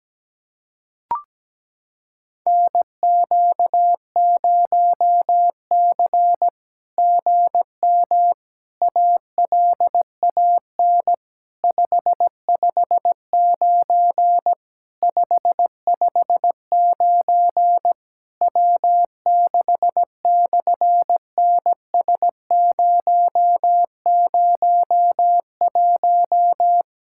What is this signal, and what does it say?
Morse code, heard as a keyed tone, about 17 wpm, 700 Hz. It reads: NQ0C GM ALAN 559 559 W6/NS001
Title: SOTA QSO Practice 17wpm